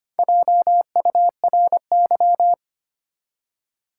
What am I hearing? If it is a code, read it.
Morse code, heard as a keyed tone, about 25 wpm, 700 Hz. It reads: JURY
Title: Four Letter Words - 25wpm